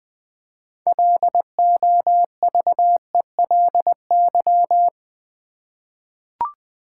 Morse code, heard as a keyed tone, about 20 wpm, 700 Hz. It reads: LOVELY
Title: 6 Letter Words - First 3 Letters Given - 20wpm